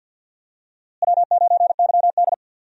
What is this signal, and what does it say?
Morse code, heard as a keyed tone, about 46 wpm, 700 Hz. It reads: W9YD
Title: USA Call Signs - 4 Characters - Repeated 3x - 50wpm